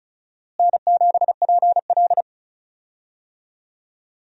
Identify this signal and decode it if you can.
Morse code, heard as a keyed tone, about 36 wpm, 700 Hz. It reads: N7PL
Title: USA Call Signs - 4 Characters - Repeated 3x - 35wpm